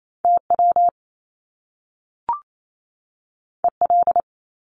Morse code, heard as a keyed tone, about 28 wpm, 700 Hz. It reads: TW EL
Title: Top Two Letter Combinations in Words 28wpm